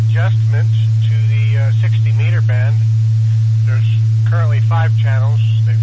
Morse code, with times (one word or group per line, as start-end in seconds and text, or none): none